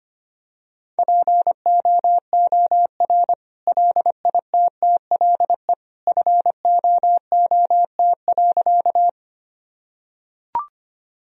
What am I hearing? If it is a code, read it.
Morse code, heard as a keyed tone, about 25 wpm, 700 Hz. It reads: POOR LITTLE FOOT.
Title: Sentences from Top 1000 Words - Review Part 3 - 25wpm